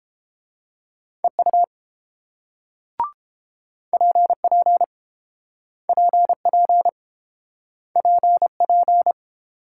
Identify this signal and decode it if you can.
Morse code, heard as a keyed tone, about 30 wpm, 700 Hz. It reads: EU PP PP PP